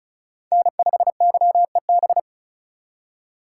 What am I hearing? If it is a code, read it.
Morse code, heard as a keyed tone, about 35 wpm, 700 Hz. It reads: N5YEB
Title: USA Call Signs - 5 Characters - 2x Character Spacing - 35wpm